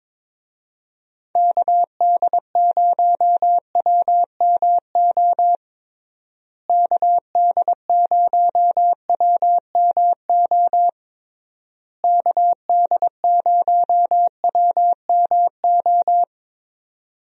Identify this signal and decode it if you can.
Morse code, heard as a keyed tone, about 22 wpm, 700 Hz. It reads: KD0WMO KD0WMO KD0WMO
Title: USA Call Signs - 6 Characters - Repeated 3x - 22wpm